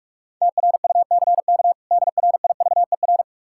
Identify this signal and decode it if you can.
Morse code, heard as a keyed tone, about 46 wpm, 700 Hz. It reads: TRUCK DRIVER
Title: Practice QSO - Exchange 3 - More Info - 45wpm